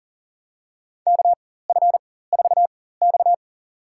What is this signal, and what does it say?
Morse code, heard as a keyed tone, about 40 wpm, spaced as for 19 wpm, 700 Hz. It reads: KF4X
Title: USA Call Signs - 4 Characters - 4x Character Spacing - 40wpm